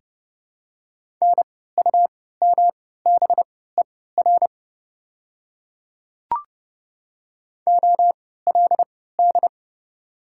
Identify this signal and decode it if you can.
Morse code, heard as a keyed tone, about 31 wpm, spaced as for 17 wpm, 690 Hz. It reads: NUMBER OLD
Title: 3x Character Spacing - Top 300 Words - 30wpm